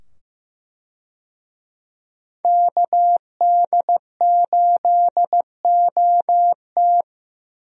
Morse code, heard as a keyed tone, about 15 wpm, 700 Hz. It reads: KD8OT
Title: USA Call Signs - 5 Characters - 15wpm